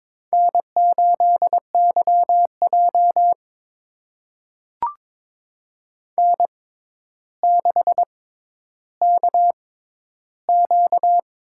Morse code, heard as a keyed tone, about 22 wpm, spaced as for 11 wpm, 700 Hz. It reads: N8YJ N6KQ
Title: USA Call Signs - 4 Characters - 6x Character Spacing - 22wpm